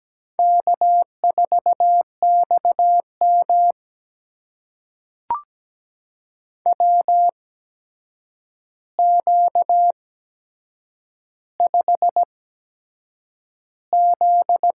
Morse code, heard as a keyed tone, about 17 wpm, spaced as for 7 wpm, 700 Hz. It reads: K4XM WQ5Z
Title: USA Call Signs - 4 Characters - 8x Character Spacing - 17wpm